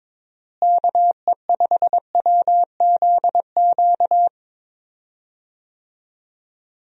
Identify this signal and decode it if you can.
Morse code, heard as a keyed tone, about 22 wpm, 700 Hz. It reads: KE5WZQ
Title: USA Call Signs - 6 Characters - Repeated 2x - 22wpm